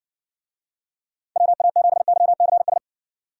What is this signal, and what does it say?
Morse code, heard as a keyed tone, about 58 wpm, 700 Hz. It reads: WA7YCS